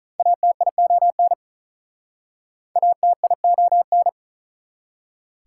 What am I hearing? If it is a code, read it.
Morse code, heard as a keyed tone, about 48 wpm, 700 Hz. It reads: ATION ATION